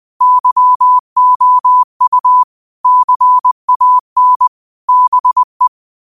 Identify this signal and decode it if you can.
Morse code, heard as a keyed tone, about 20 wpm, 1000 Hz. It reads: YOU CAN BE